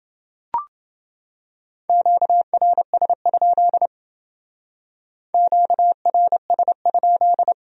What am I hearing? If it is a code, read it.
Morse code, heard as a keyed tone, about 28 wpm, 700 Hz. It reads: QRS? QRS?